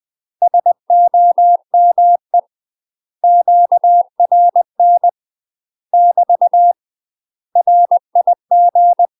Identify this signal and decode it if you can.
Morse code, heard as a keyed tone, about 20 wpm, 700 Hz. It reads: SOME QRN <BT> RIG
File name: UK MASTER QSOs TRAINING / 20WPM / UK MASTER QSOs 0003